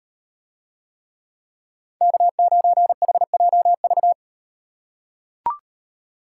Morse code, heard as a keyed tone, about 36 wpm, 700 Hz. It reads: K9HJV